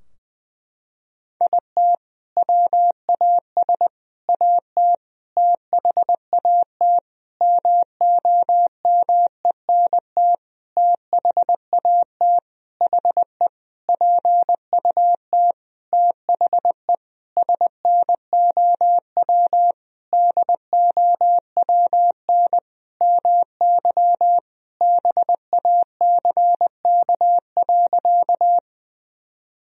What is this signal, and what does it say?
Morse code, heard as a keyed tone, about 20 wpm, 700 Hz. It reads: IT WAS AT THAT MOMENT THAT HE PUT THE SNOW DOWN MY BACK.
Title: Sentences from Top 1000 Words - Review Part 3 - 20wpm